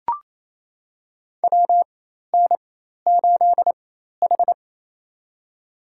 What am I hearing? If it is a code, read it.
Morse code, heard as a keyed tone, about 28 wpm, spaced as for 13 wpm, 700 Hz. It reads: WN8H
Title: USA Call Signs - 4 Characters - 4x Character Spacing - 28wpm